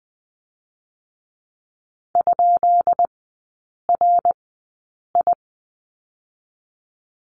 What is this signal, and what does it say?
Morse code, heard as a keyed tone, about 20 wpm, 700 Hz. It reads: ? R I
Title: 133 Lesson 22e - Question Mark - Three Characters 20wpm